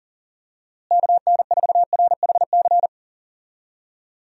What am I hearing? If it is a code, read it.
Morse code, heard as a keyed tone, about 40 wpm, 700 Hz. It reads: KN4RHC
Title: USA Call Signs - 6 Characters - Repeated 3x - 40wpm